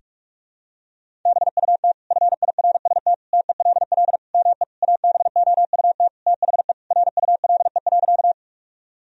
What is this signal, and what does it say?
Morse code, heard as a keyed tone, about 46 wpm, 700 Hz. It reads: BUT FIRST TELL ME ABOUT THE RULE.
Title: Sentences from Top 1000 Words - Review Part 2 - 45wpm